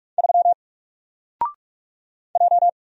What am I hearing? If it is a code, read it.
Morse code, heard as a keyed tone, about 48 wpm, 700 Hz. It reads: 3 J
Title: Single Letter-Number Alt 45wpm